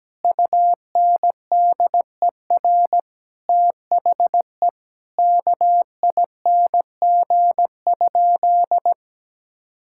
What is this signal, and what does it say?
Morse code, heard as a keyed tone, about 17 wpm, 700 Hz. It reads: UNDER THE KING?
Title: Sentences from Top 1000 Words - Review Part 6 - 17wpm